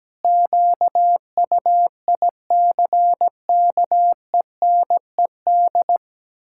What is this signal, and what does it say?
Morse code, heard as a keyed tone, about 17 wpm, 700 Hz. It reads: QUICKENED
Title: Pangram Challenge v2 - 17wpm